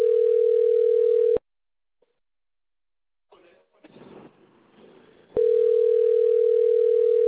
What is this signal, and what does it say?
Ringback tone (North America), 440 + 480 Hz.